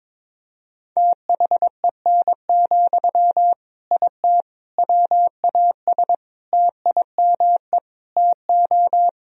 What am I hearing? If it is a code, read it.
Morse code, heard as a keyed tone, about 22 wpm, 700 Hz. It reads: THEN, IT WAS TIME TO
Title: Sentences from Top 1000 Words - Review Part 1 - 22wpm